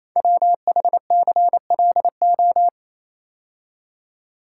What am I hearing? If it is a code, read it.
Morse code, heard as a keyed tone, about 29 wpm, 700 Hz. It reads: WHCLO
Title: Code Groups - 5 Letter:Numbers - 28wpm